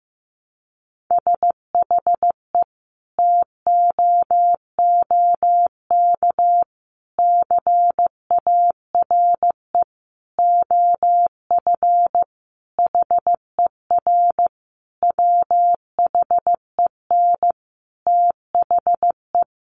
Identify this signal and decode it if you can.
Morse code, heard as a keyed tone, about 15 wpm, 700 Hz. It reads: SHE TOOK CARE OF HER WHEN THE
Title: Sentences from Top 1300 Words 15wpm